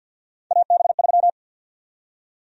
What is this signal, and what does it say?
Morse code, heard as a keyed tone, about 53 wpm, 700 Hz. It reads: AB3
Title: USA Call Sign Prefixes - 50wpm